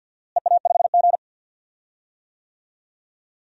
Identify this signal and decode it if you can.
Morse code, heard as a keyed tone, about 46 wpm, 700 Hz. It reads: EA5G
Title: DXCC-100 Call Signs - 4 Characters - Repeated 2x - 50wpm